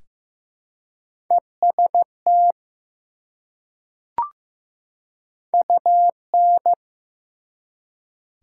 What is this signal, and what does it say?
Morse code, heard as a keyed tone, about 15 wpm, 700 Hz. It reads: EST UN
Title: Prefixes and Suffixes 15wpm